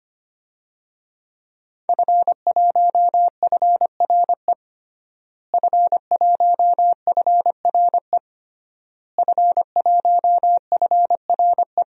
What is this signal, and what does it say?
Morse code, heard as a keyed tone, about 25 wpm, 700 Hz. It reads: F1FRE F1FRE F1FRE
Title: DXCC-100 Call Signs - 5 Characters - Repeated 3x - 25wpm